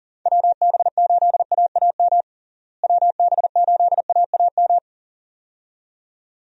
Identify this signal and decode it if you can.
Morse code, heard as a keyed tone, about 40 wpm, 700 Hz. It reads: WB8AAM WB8AAM
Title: USA Call Signs - 6 Characters - Repeated 2x - 40wpm